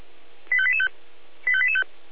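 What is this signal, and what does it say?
Five-tone selective call (ZVEI), heard as quick runs of five tones, tones of about 70 ms. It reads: 85885 85885
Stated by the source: ZVEI five-tone selective call received by SDR.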